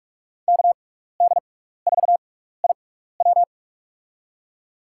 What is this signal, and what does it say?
Morse code, heard as a keyed tone, about 43 wpm, spaced as for 15 wpm, 700 Hz. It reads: KD4IW
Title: USA Call Signs - 5 Characters - 6x Character Spacing - 45wpm